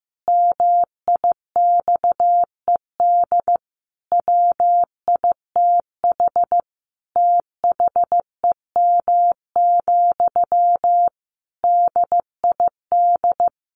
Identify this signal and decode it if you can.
Morse code, heard as a keyed tone, about 14 wpm, 700 Hz. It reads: MIXED WITH THEM, DID
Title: Sentences from Top 1500 Words 15wpm